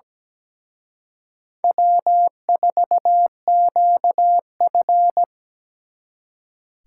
Morse code, heard as a keyed tone, about 20 wpm, 700 Hz. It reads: W4QF
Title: Speed Racing - USA 1x2 Call Signs 17wpm